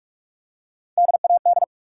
Morse code, heard as a keyed tone, about 44 wpm, 700 Hz. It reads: DAD
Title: Three Letter Words - 45wpm